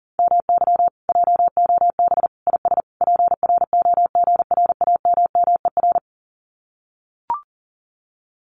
Morse code, heard as a keyed tone, about 40 wpm, 700 Hz. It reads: MY JOB IS PROGRAMMER
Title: Common QSO Phrases Encore 40wpm